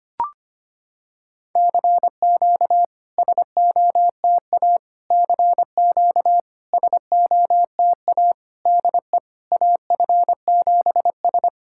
Morse code, heard as a keyed tone, about 25 wpm, 690 Hz. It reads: CQ SOTA CQ SOTA DE AF7S
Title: SOTA QSO Practice 25wpm